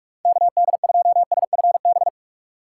Morse code, heard as a keyed tone, about 46 wpm, 700 Hz. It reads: KD2SFB
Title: USA Call Signs - 3x Character Spacing - 45wpm